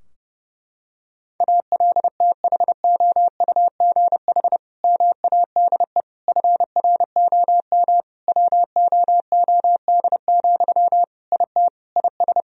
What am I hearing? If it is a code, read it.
Morse code, heard as a keyed tone, about 30 wpm, 700 Hz. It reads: ALTHOUGH MADE FROM WOOD, IT IS
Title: Sentences from Top 1000 Words - Review Part 3 - 30wpm